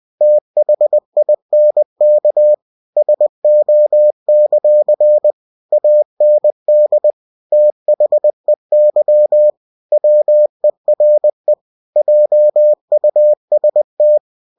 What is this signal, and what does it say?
Morse code, heard as a keyed tone, about 20 wpm, 600 Hz. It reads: THINK SO; AND THEY WERE JUST
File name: cw-books / winnie-the-pooh / winnie-20wpm-page-0037